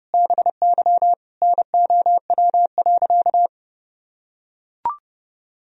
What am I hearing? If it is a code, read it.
Morse code, heard as a keyed tone, about 30 wpm, 700 Hz. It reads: BY NOW.
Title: Sentences from Top 1000 Words - Review Part 3 - 30wpm